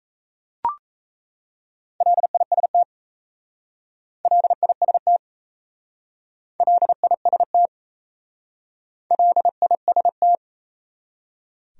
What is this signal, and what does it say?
Morse code, heard as a keyed tone, about 37 wpm, 700 Hz. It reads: LIST LIST LIST LIST